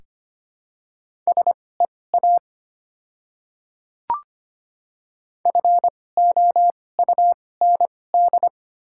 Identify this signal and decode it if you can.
Morse code, heard as a keyed tone, about 25 wpm, spaced as for 18 wpm, 690 Hz. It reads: SEA FOUND
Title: 2x Character Spacing - Top 300 Words - 25wpm